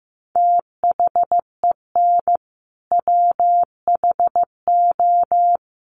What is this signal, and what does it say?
Morse code, heard as a keyed tone, about 15 wpm, 700 Hz. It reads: THEN WHO
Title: Sentences from Top 100 Words 15wpm